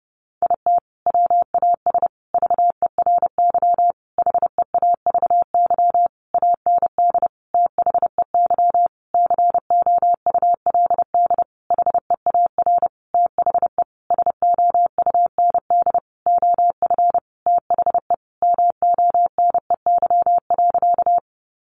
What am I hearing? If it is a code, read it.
Morse code, heard as a keyed tone, about 30 wpm, 700 Hz. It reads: IT WAS VERY HEAVY AND THEY COULD HEAR THE SOUND OF THE MONEY.
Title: Sentences from Top 1100 Words 30wpm